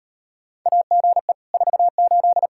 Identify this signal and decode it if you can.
Morse code, heard as a keyed tone, about 39 wpm, 700 Hz. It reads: AGE 48